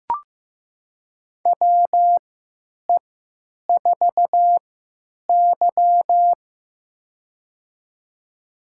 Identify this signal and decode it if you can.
Morse code, heard as a keyed tone, about 15 wpm, spaced as for 9 wpm, 700 Hz. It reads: WE4Y
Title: USA Call Signs - 3x Character Spacing - 15wpm